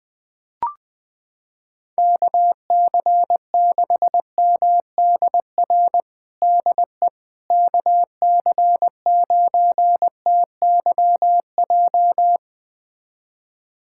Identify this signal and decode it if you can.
Morse code, heard as a keyed tone, about 21 wpm, 700 Hz. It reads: KC6MDR DE KC9TYJ
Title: USA Callsign DE Callsign - 6 Characters - 20wpm